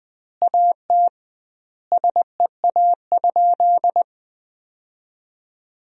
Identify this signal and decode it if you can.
Morse code, heard as a keyed tone, about 21 wpm, 700 Hz. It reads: AT SEA?
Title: 2x Word-Spacing - Sentences from Top 300 Words - 20wpm